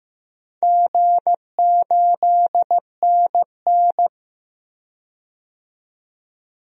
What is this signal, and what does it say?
Morse code, heard as a keyed tone, about 15 wpm, 700 Hz. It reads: G8NN
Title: DXCC-100 Call Signs - 4 Characters - 15wpm